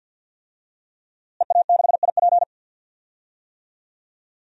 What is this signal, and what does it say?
Morse code, heard as a keyed tone, about 51 wpm, 700 Hz. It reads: EA6IP
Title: DXCC-100 Call Signs - 5 Characters - Repeated 3x - 50wpm